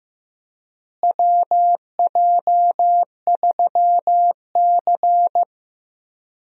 Morse code, heard as a keyed tone, about 15 wpm, 700 Hz. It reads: WJ3C
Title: USA Call Signs - 4 Characters - Repeated 2x - 15wpm